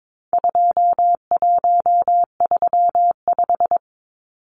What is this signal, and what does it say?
Morse code, heard as a keyed tone, about 22 wpm, 700 Hz. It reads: 2135
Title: Numbers - 4-Digit 22wpm